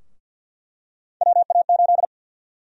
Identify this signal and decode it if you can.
Morse code, heard as a keyed tone, about 50 wpm, 700 Hz. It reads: WA8
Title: USA Call Sign Prefixes - 50wpm